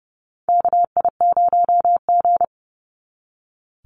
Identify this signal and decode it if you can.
Morse code, heard as a keyed tone, about 30 wpm, 700 Hz. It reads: KI0G
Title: NA SOTA Chaser Callsigns 30wpm